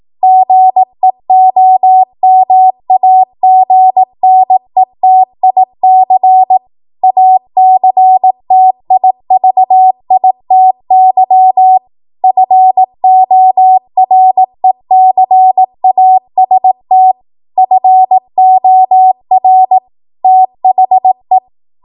Morse code, heard as a keyed tone, about 18 wpm, 750 Hz. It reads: GEOMAGNETIC ACTIVITY FORECAST FOR THE
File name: Archive / 18 WPM / 18WPM